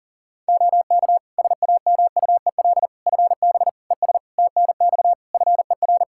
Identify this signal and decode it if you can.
Morse code, heard as a keyed tone, about 40 wpm, 700 Hz. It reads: OK SAMUEL FB ES TNX FER
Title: K1USN SST QSO Practice 40wpm